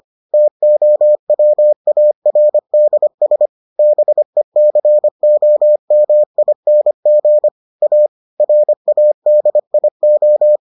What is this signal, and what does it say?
Morse code, heard as a keyed tone, about 25 wpm, 600 Hz. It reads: TOWARDS BECOMING A RADIO